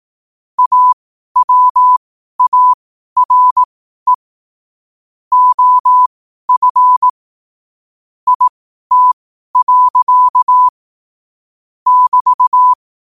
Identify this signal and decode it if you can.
Morse code, heard as a keyed tone, about 18 wpm, spaced as for 12 wpm, 1000 Hz. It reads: AWARE OF IT. <BT>
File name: qotdfiles / 12wpm